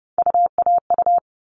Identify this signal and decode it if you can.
Morse code, heard as a keyed tone, about 30 wpm, 700 Hz. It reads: UAU
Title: All Cut Numbers - 3-Digit 30wpm